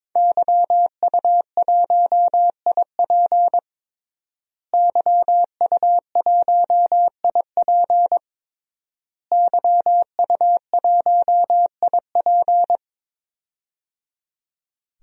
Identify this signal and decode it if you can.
Morse code, heard as a keyed tone, about 22 wpm, 700 Hz. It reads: YU1IP YU1IP YU1IP
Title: DXCC-100 Call Signs - 5 Characters - Repeated 3x - 22wpm